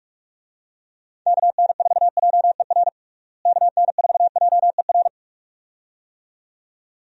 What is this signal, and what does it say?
Morse code, heard as a keyed tone, about 42 wpm, 700 Hz. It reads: KN4JER KN4JER
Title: USA Call Signs - 6 Characters - Repeated 2x - 45wpm